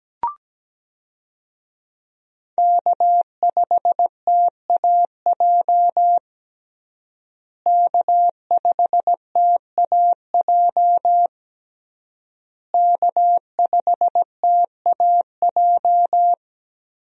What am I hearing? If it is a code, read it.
Morse code, heard as a keyed tone, about 17 wpm, 700 Hz. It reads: K5TAJ K5TAJ K5TAJ
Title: USA Call Signs - 5 Characters - Repeated 3x - 17wpm